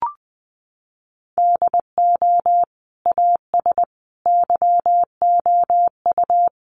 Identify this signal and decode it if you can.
Morse code, heard as a keyed tone, about 20 wpm, 700 Hz. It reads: DO AS YOU
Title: Sentences from Top 100 Words 20wpm